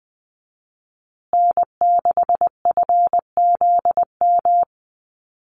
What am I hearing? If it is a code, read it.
Morse code, heard as a keyed tone, about 20 wpm, 700 Hz. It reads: N6FZM
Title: USA Call Signs Repeated 2x 20wpm